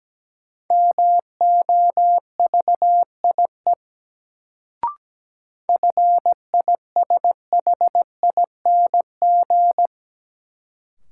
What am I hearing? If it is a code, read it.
Morse code, heard as a keyed tone, about 17 wpm, 700 Hz. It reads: MOVIE FISHING
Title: Sentences from Top 1000 Words - Review Part 5 - New Word Drill - 17wpm